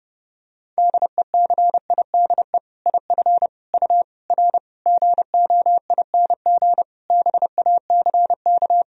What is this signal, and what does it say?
Morse code, heard as a keyed tone, about 30 wpm, 700 Hz. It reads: DECIDE IF U R GOING BACK
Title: Sentences from Top 500 Words and Abbreviations 30wpm